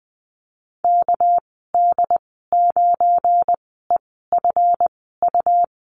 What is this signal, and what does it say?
Morse code, heard as a keyed tone, about 20 wpm, spaced as for 14 wpm, 700 Hz. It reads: KD9EFU
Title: 231 Lesson 37g - Letter x - Call Signs 20wpm